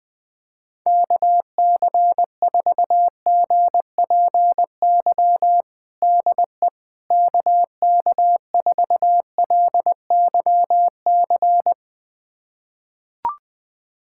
Morse code, heard as a keyed tone, about 20 wpm, 700 Hz. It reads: KC4GPY DE KK4LYC
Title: USA Callsign DE Callsign - 6 Characters - 20wpm